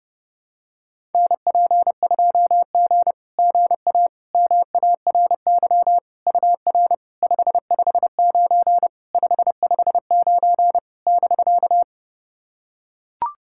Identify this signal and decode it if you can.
Morse code, heard as a keyed tone, about 30 wpm, 700 Hz. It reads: NP2G GA MARY UR 559 559 <BK>
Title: SOTA QSO Practice 30wpm